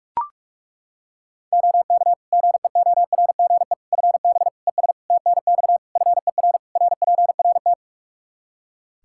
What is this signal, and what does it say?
Morse code, heard as a keyed tone, about 41 wpm, 700 Hz. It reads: OK GEORGE FB ES TNX FER RPRT
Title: K1USN SST QSO Practice 45wpm